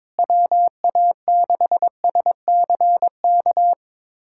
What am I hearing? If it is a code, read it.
Morse code, heard as a keyed tone, about 22 wpm, 700 Hz. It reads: WA6SCK
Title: USA Callsign DE Callsign - 6 Characters - 22wpm